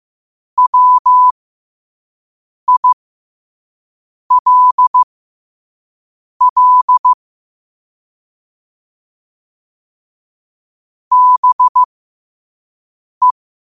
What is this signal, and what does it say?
Morse code, heard as a keyed tone, about 15 wpm, spaced as for 5 wpm, 1000 Hz. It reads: WILL BE
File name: qotdfiles / 5wpm